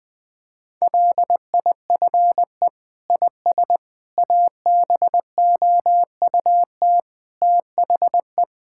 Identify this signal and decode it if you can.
Morse code, heard as a keyed tone, about 20 wpm, 700 Hz. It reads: LIFE IS ABOUT THE
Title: Sentences with 6-Word Phrases - Spoken Beginning - 20wpm